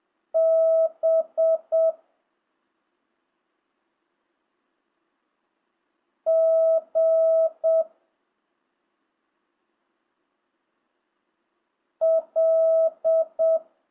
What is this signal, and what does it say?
Morse code, heard as a keyed tone, about 7 wpm, 650 Hz. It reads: B G L
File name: jh8cbh / 2025cw